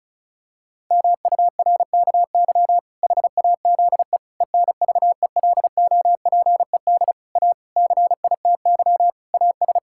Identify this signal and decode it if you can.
Morse code, heard as a keyed tone, about 36 wpm, 700 Hz. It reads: MURKY HAZE ENVELOPED A CITY AS
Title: Pangram Challenge v2 - 35wpm